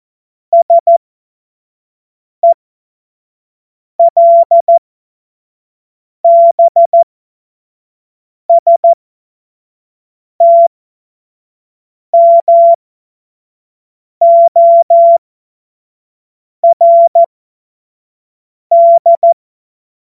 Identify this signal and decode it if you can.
Morse code, heard as a keyed tone, about 14 wpm, spaced as for 5 wpm, 680 Hz. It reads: SELBSTMORD
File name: Morsekurs / Witze / Witz 16 Januar 2024 25 F70